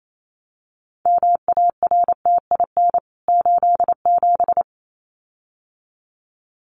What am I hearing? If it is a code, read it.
Morse code, heard as a keyed tone, about 28 wpm, 700 Hz. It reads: MARTIN 87
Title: CWT Exchange 28wpm